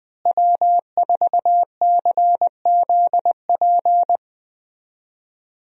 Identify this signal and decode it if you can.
Morse code, heard as a keyed tone, about 20 wpm, 700 Hz. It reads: W4CZP
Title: USA Call Signs - 5 Characters - Repeated 3x - 20wpm